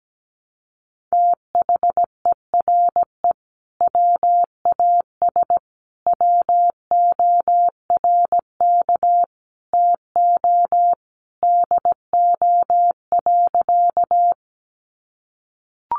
Morse code, heard as a keyed tone, about 17 wpm, 700 Hz. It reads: THERE WAS WORK TO DO.
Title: Sentences from Top 200 Words 17wpm